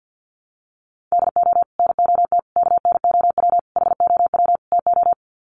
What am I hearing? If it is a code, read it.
Morse code, heard as a keyed tone, about 50 wpm, 700 Hz. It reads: DO NOT KNOW HOW TO
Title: Sets of 5 Words 50wpm